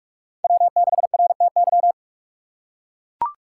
Morse code, heard as a keyed tone, about 41 wpm, 700 Hz. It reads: W6RTY
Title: USA Call Signs - 6x Character Spacing - 45wpm